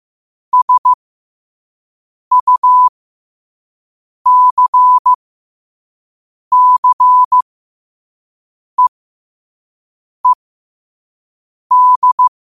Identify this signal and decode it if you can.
Morse code, heard as a keyed tone, about 15 wpm, spaced as for 5 wpm, 1000 Hz. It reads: SUCCEED
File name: qotdfiles / 5wpm